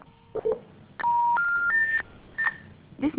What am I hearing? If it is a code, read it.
Special information tone, with speech.